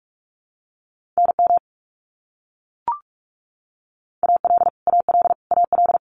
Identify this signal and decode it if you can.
Morse code, heard as a keyed tone, about 44 wpm, 700 Hz. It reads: NM AL AL AL
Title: US States Repeated 3x 45wpm